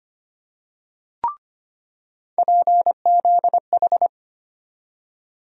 Morse code, heard as a keyed tone, about 25 wpm, 700 Hz. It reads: PZH
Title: USA Call Sign Postfixes - 3 Letters 25wpm